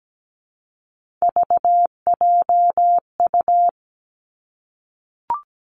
Morse code, heard as a keyed tone, about 17 wpm, 700 Hz. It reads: VJU
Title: Three Challenging Letter-Number Combinations 17wpm